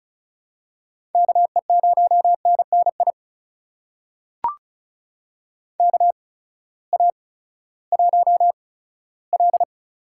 Morse code, heard as a keyed tone, about 34 wpm, spaced as for 17 wpm, 700 Hz. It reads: KE0NNI KA1L
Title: USA Call Signs - 6 Characters - 8x Character Spacing - 35wpm